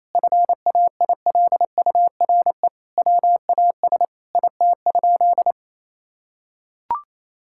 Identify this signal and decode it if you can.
Morse code, heard as a keyed tone, about 28 wpm, 700 Hz. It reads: FAILURE WAS IT?
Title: Sentences from Top 1000 Words - Review Part 6 - 28wpm